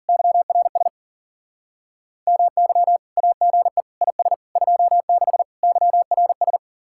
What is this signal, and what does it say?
Morse code, heard as a keyed tone, about 48 wpm, 700 Hz. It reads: YRS MY AGE IS 26 YRS
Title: Speed-Racing Common QSO Phrases Encore - 40wpm